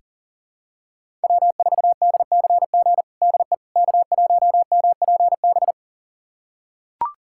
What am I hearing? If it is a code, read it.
Morse code, heard as a keyed tone, about 37 wpm, 700 Hz. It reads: W4DCG DE K1MPB
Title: USA Callsign DE Callsign - 5 Characters - 40wpm